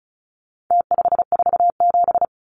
Morse code, heard as a keyed tone, about 34 wpm, 700 Hz. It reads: T547
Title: Numbers - Contest Style - 4-Digit 35wpm